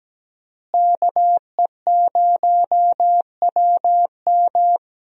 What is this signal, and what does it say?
Morse code, heard as a keyed tone, about 17 wpm, 700 Hz. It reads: KE0WM
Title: USA Call Signs - 5 Characters - Repeated 3x - 17wpm